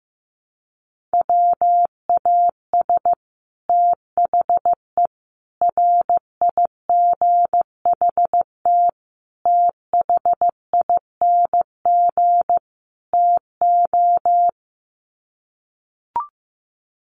Morse code, heard as a keyed tone, about 14 wpm, 700 Hz. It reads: WAS THE RIGHT THING TO
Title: Sets of 5 Words Encore2 15wpm